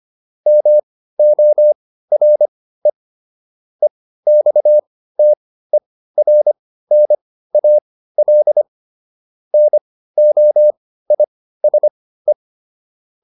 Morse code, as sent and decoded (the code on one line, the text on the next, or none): -- --- .-. / . -..- - . .-. -. .- .-.. / -. --- .. ... .
MORE EXTERNAL NOISE